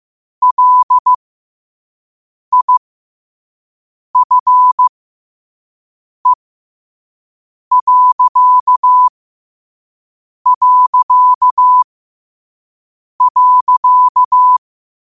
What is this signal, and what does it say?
Morse code, heard as a keyed tone, about 15 wpm, spaced as for 5 wpm, 1000 Hz. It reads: LIFE...
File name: qotdfiles / 5wpm